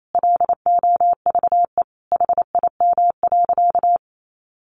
Morse code, heard as a keyed tone, about 28 wpm, 700 Hz. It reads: LOVE HIM.
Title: Sentences from Top 1100 Words 28wpm